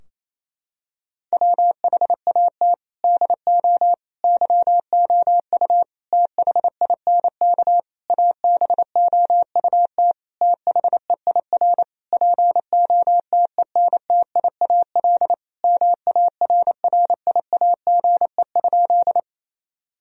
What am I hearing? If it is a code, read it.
Morse code, heard as a keyed tone, about 28 wpm, 700 Hz. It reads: WHAT DO YOU THINK ABOUT THEIR POTENTIAL MARRIAGE?
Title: Sentences from Top 1000 Words - Review Part 4 - 28wpm